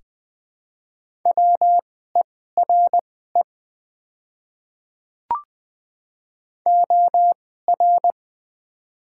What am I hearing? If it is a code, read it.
Morse code, heard as a keyed tone, about 20 wpm, spaced as for 14 wpm, 700 Hz. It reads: WERE OR
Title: 2x Character Spacing - Top 200 Words - 20wpm